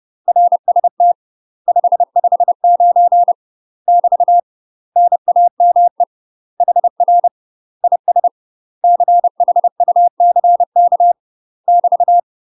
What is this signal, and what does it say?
Morse code, heard as a keyed tone, about 30 wpm, 700 Hz. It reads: RST 559 <BT> NAME HR IS CHUCK <BT>
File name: UK MASTER QSOs TRAINING / 30WPM / UK MASTER QSOs 0017